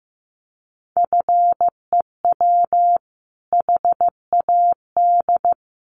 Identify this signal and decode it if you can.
Morse code, heard as a keyed tone, about 15 wpm, 700 Hz. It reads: FEW HAD
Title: Sentences from Top 300 Words 15wpm